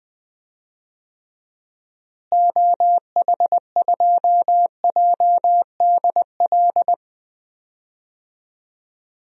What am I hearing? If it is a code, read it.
Morse code, heard as a keyed tone, about 20 wpm, 700 Hz. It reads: OH2JDL
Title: DXCC-100 Call Signs - 6 Characters - Repeated 2x - 20wpm